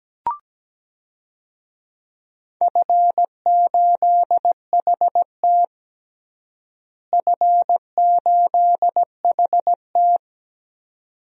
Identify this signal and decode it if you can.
Morse code, heard as a keyed tone, about 17 wpm, 700 Hz. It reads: F8HT F8HT
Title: DXCC-100 Call Signs - 4 Characters - Repeated 2x - 17wpm